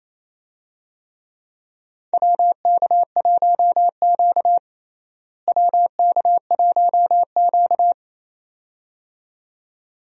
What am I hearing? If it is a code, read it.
Morse code, heard as a keyed tone, about 28 wpm, 700 Hz. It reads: WK1Q WK1Q
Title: USA Call Signs - 4 Characters - Repeated 2x - 28wpm